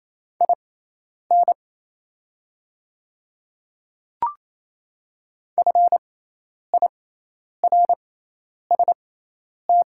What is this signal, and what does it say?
Morse code, heard as a keyed tone, about 28 wpm, spaced as for 10 wpm, 690 Hz. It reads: IN FIRST
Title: 6x Character Spacing - Top 500 Words - 28wpm